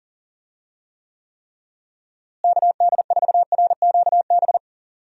Morse code, heard as a keyed tone, about 40 wpm, 700 Hz. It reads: KD4RQB
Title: USA Call Signs - 6 Characters - Repeated 3x - 40wpm